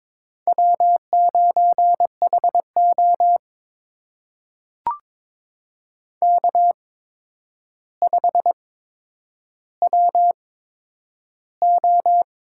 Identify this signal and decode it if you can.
Morse code, heard as a keyed tone, about 22 wpm, spaced as for 9 wpm, 700 Hz. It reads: W9HO K5WO
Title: USA Call Signs - 4 Characters - 8x Character Spacing - 22wpm